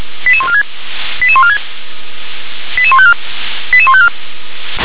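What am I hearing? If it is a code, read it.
Five-tone selective call (ZVEI), heard as quick runs of five tones, tones of about 80 ms. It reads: 88146 88146 88154 88154
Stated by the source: ZVEI five-tone selective call 88084, received on SDR.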